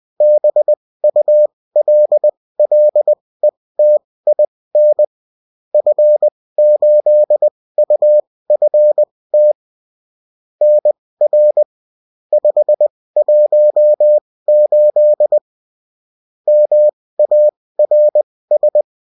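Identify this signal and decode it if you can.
Morse code, heard as a keyed tone, about 20 wpm, spaced as for 16 wpm, 600 Hz. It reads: BULLETIN F8UFT NR 518 MARS
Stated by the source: Morse code bulletin of a French amateur CW union, sent as text and in CW.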